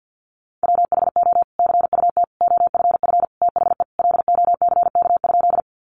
Morse code, heard as a keyed tone, about 50 wpm, 700 Hz. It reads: WHO CUT OFF THE LOCK?
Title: Sentences from Top 1300 Words 50wpm